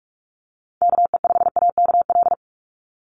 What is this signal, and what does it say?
Morse code, heard as a keyed tone, about 46 wpm, 690 Hz. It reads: KE5AKL
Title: NA SOTA Chaser Callsigns 45wpm